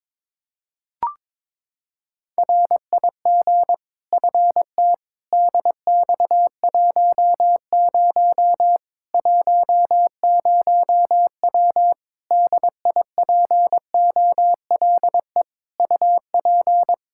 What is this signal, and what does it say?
Morse code, heard as a keyed tone, about 22 wpm, 700 Hz. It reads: RIG FT DX10 10W DIPOLE UP
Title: Practice QSO - Exchange 2 - Info - 22wpm